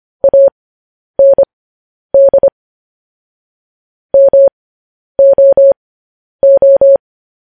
Morse code, heard as a keyed tone, about 25 wpm, spaced as for 10 wpm, 550 Hz. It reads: AND MOO